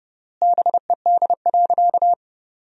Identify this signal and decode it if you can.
Morse code, heard as a keyed tone, about 30 wpm, 700 Hz. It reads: BED.
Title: 3x Word-Spacing - Sentences from Top 500 Words - 30wpm